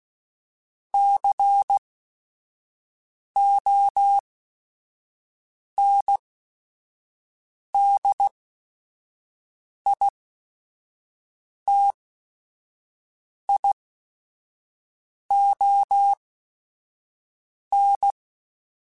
Morse code, heard as a keyed tone, about 16 wpm, spaced as for 5 wpm, 800 Hz. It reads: CONDITION
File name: Technical Subjects / Technical Subjects-mp3s / AND Gate @5